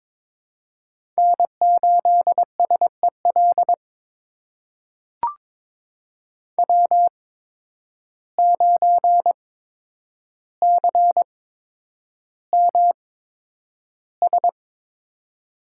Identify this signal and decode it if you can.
Morse code, heard as a keyed tone, about 22 wpm, spaced as for 9 wpm, 700 Hz. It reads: N8SEL W9CMS
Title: USA Call Signs - 5 Characters - 8x Character Spacing - 22wpm